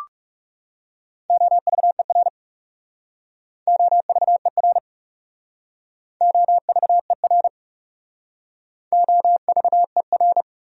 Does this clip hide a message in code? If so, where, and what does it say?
Yes: Morse code from 1.3 to 10.4 s, heard as a keyed tone, about 40 wpm, 700 Hz. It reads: OVER OVER OVER OVER